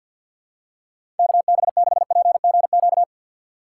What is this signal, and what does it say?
Morse code, heard as a keyed tone, about 50 wpm, 700 Hz. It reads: KB6PGQ
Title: USA Call Signs - 6 Characters - 3x Character Spacing - 50wpm